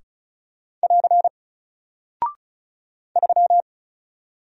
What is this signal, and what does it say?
Morse code, heard as a keyed tone, about 36 wpm, 700 Hz. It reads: <AR> 3
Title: Single Character (Letter:Number:Punctuation:Prosign) 35wpm